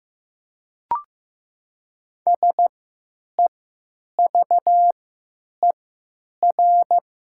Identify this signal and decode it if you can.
Morse code, heard as a keyed tone, about 15 wpm, 700 Hz. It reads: S E V E R
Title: 3x Character Spacing - Top 500 Words - 15wpm